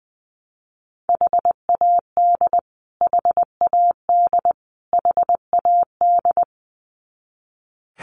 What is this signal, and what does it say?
Morse code, heard as a keyed tone, about 20 wpm, 700 Hz. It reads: HAD HAD HAD
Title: Top 500 Words Repeated 20wpm